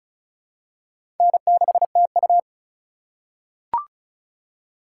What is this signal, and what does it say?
Morse code, heard as a keyed tone, about 36 wpm, 700 Hz. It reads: N6TU
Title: USA Call Signs - 4 Characters - 4x Character Spacing - 35wpm